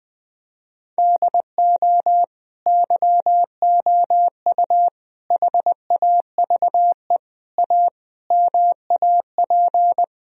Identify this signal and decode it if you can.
Morse code, heard as a keyed tone, about 20 wpm, 700 Hz. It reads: DO YOU HAVE A MAP
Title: Sentences from Top 1000 Words - Review Part 2 - 20wpm